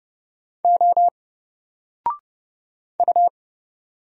Morse code, heard as a keyed tone, about 30 wpm, 700 Hz. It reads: O U
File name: Single-Letter-Number-30wpm